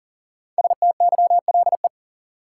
Morse code, heard as a keyed tone, about 40 wpm, 700 Hz. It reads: STYLE
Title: Sentences from Top 800 Words - New Word Drill - 40wpm